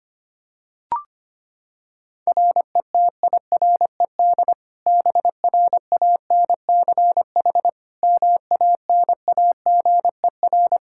Morse code, heard as a keyed tone, about 25 wpm, 700 Hz. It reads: RETIRED BRANCH MANAGER
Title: Practice QSO - Exchange 3 - More Info - 25wpm